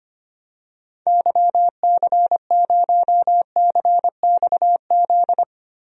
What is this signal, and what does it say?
Morse code, heard as a keyed tone, about 25 wpm, 700 Hz. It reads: YC0CXZ
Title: DXCC-100 Call Signs - 6 Characters - Repeated 2x - 25wpm